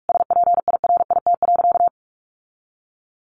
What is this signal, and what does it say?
Morse code, heard as a keyed tone, about 44 wpm, 700 Hz. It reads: SPIRIT.
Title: Sentences from Top 1400 Words 45wpm